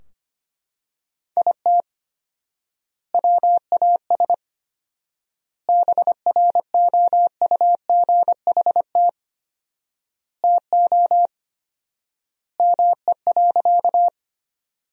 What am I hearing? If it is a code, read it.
Morse code, heard as a keyed tone, about 25 wpm, 700 Hz. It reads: IT WAS BROUGHT TO ME.
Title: 4x Word-Spacing - Sentences from Top 500 Words - 25wpm